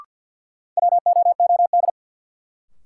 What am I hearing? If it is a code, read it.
Morse code, heard as a keyed tone, about 52 wpm, 700 Hz. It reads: WOOD
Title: Sentences from Top 1000 Words - Review Part 2 - New Word Drill - 50wpm